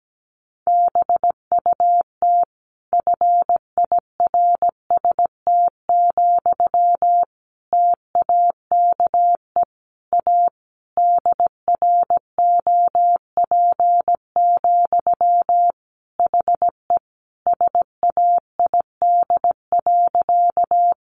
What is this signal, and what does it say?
Morse code, heard as a keyed tone, about 17 wpm, 700 Hz. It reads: BUT FIRST, TAKE A DROP, HE SAID.
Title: Sentences from Top 1400 Words 17wpm